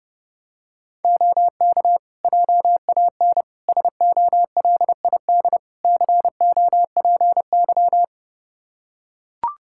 Morse code, heard as a keyed tone, about 30 wpm, 700 Hz. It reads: OK JAN SOLID COPY
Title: K1USN SST QSO Practice 30wpm